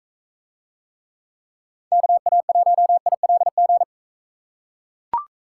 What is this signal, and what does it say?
Morse code, heard as a keyed tone, about 41 wpm, 700 Hz. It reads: KA1ILG